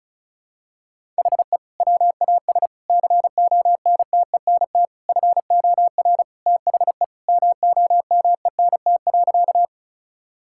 Morse code, heard as a keyed tone, about 34 wpm, 700 Hz. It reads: HE WAS CONTENT FOR THE MOMENT.
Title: Sentences from Top 1000 Words - Review Part 6 - 35wpm